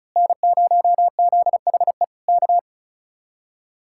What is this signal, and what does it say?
Morse code, heard as a keyed tone, about 32 wpm, 700 Hz. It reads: N0ZHE K
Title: Speed-Racing Common QSO Phrases - 35wpm